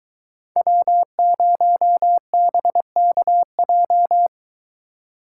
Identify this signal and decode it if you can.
Morse code, heard as a keyed tone, about 23 wpm, 700 Hz. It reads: W0BKJ